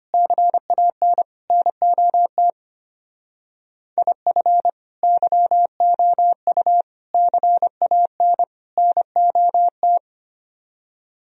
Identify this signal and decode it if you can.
Morse code, heard as a keyed tone, about 31 wpm, 700 Hz. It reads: CAN NOT IF YOU CAN NOT
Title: Speed-Racing - Sets of 4 Words 25wpm